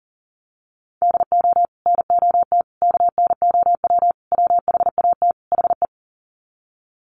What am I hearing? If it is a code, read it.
Morse code, heard as a keyed tone, about 40 wpm, 700 Hz. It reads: DO NOT KNOW WHAT HE
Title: Sets of 5 Words Encore2 40wpm